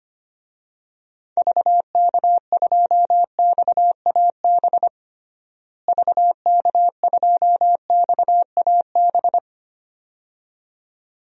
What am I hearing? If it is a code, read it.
Morse code, heard as a keyed tone, about 25 wpm, 700 Hz. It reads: VK2XAB VK2XAB
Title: DXCC-100 Call Signs - 6 Characters - Repeated 2x - 25wpm